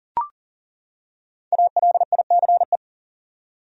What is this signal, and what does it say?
Morse code, heard as a keyed tone, about 40 wpm, 700 Hz. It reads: ALICE
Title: USA Common Names - Max 5 Letters - 40wpm